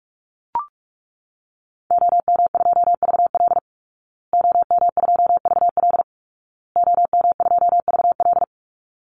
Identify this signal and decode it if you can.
Morse code, heard as a keyed tone, about 44 wpm, 700 Hz. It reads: OM2VL OM2VL OM2VL
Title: Call Signs Repeated 2x - 45wpm